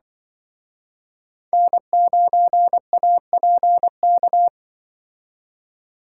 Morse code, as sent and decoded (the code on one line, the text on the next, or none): -. ----. .- .--. -.-
N9APK